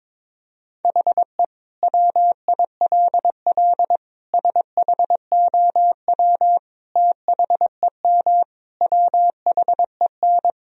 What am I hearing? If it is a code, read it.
Morse code, heard as a keyed tone, about 22 wpm, 700 Hz. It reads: HE WILL SHOW THEM WHEN
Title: Sentences from Top 1000 Words - Review Part 1 - 22wpm